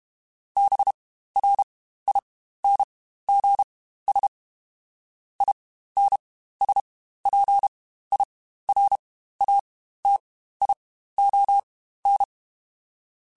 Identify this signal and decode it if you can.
Morse code, heard as a keyed tone, about 32 wpm, spaced as for 15 wpm, 800 Hz. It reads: BRINGS INSPIRATION
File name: Quotes Spaced / Quotes Spaced-mp3s / Persperation sp @30